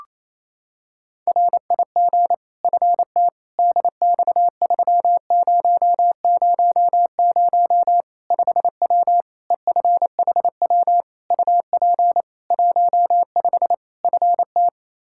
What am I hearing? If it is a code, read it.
Morse code, heard as a keyed tone, about 28 wpm, 700 Hz. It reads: RIG FT DX3000 5W EFHW UP 15 FT
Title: K1USN SST QSO Practice 28wpm